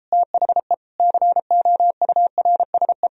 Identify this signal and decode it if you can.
Morse code, heard as a keyed tone, about 38 wpm, 700 Hz. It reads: THE COURSE